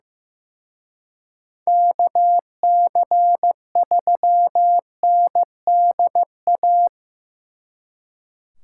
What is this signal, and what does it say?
Morse code, heard as a keyed tone, about 18 wpm, 700 Hz. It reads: KC3NDA
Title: Speed Racing - USA 2x3 Call Signs 15wpm